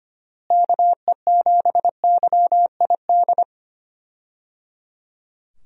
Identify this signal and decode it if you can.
Morse code, heard as a keyed tone, about 25 wpm, 700 Hz. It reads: KE7YID
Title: USA Call Signs - 6 Characters - Repeated 3x - 25wpm